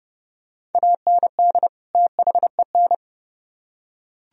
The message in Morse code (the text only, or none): AND THEN